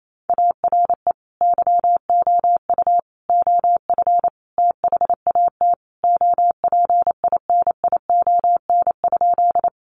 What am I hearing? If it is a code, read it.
Morse code, heard as a keyed tone, about 28 wpm, 700 Hz. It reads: ARE YOU OF THAT OPINION?
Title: Sentences from Top 1200 Words 28wpm